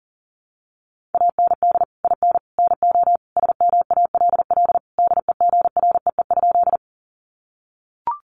Morse code, heard as a keyed tone, about 40 wpm, 700 Hz. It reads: AND IN NO SMALL DEGREE?
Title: Sentences from Top 1500 Words 40wpm